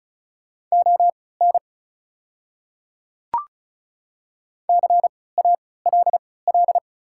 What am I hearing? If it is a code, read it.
Morse code, heard as a keyed tone, about 36 wpm, spaced as for 20 wpm, 690 Hz. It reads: ON CALL
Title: 3x Character Spacing - Top 100 Words - 35wpm